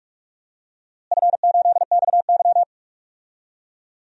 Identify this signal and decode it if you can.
Morse code, heard as a keyed tone, about 47 wpm, 700 Hz. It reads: F8XY
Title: DXCC-100 Call Signs - 4 Characters - Repeated 2x - 45wpm